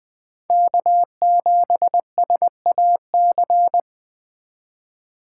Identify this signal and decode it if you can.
Morse code, heard as a keyed tone, about 20 wpm, 700 Hz. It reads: K7SAC
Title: Speed Racing - USA 1x3 Call Signs 17wpm